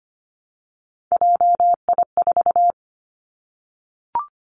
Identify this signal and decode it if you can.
Morse code, heard as a keyed tone, about 25 wpm, 700 Hz. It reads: JI4
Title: Three Challenging Letter-Number Combinations 25wpm